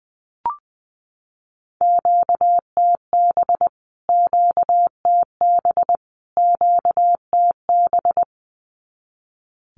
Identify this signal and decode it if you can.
Morse code, heard as a keyed tone, about 20 wpm, 700 Hz. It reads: QTB QTB QTB
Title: Q-Codes Repeated 3x 20wpm